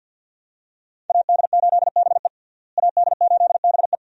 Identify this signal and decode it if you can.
Morse code, heard as a keyed tone, about 52 wpm, 700 Hz. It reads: AD8BE AD8BE
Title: USA Call Signs - 5 Characters - Repeated 2x - 50wpm